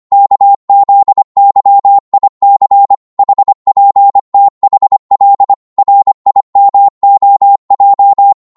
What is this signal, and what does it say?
Morse code, heard as a keyed tone, about 25 wpm, 800 Hz. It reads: KZYIC HPTHL RIMOJ
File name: Morse / ltr-25wpm